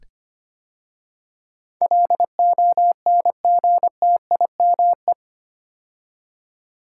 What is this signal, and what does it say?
Morse code, heard as a keyed tone, about 25 wpm, 700 Hz. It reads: LONGTIME
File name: 8 Letter Words - First 3 Letters Given - 25wpm